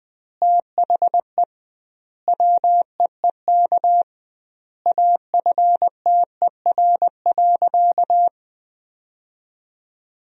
Morse code, heard as a keyed tone, about 20 wpm, 700 Hz. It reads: THE WEEK AFTER.
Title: 2x Word-Spacing - Sentences from Top 500 Words - 20wpm